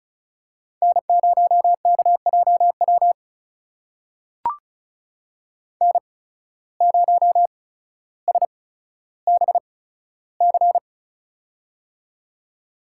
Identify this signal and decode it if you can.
Morse code, heard as a keyed tone, about 36 wpm, spaced as for 15 wpm, 700 Hz. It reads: N0KJW N0SBC
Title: USA Call Signs - 5 Characters - 8x Character Spacing - 35wpm